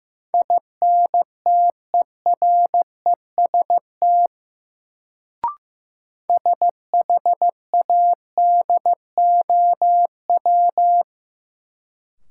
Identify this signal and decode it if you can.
Morse code, heard as a keyed tone, about 15 wpm, 700 Hz. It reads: INTEREST SHADOW
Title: Sets of 5 Words Encore - New Word Drill - 15wpm